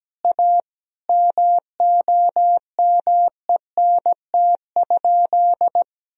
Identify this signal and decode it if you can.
Morse code, heard as a keyed tone, about 17 wpm, 700 Hz. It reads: A MOMENT?
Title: Sentences from Top 1000 Words - Review Part 1 - 17wpm